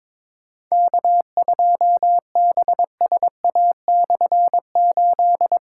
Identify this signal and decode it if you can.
Morse code, heard as a keyed tone, about 22 wpm, 700 Hz. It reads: K2BSA/8
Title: K1USN SST QSO Practice - 22wpm